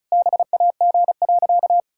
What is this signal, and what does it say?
Morse code, heard as a keyed tone, about 34 wpm, 700 Hz. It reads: BAG.
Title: Pangram Challenge v2 - 35wpm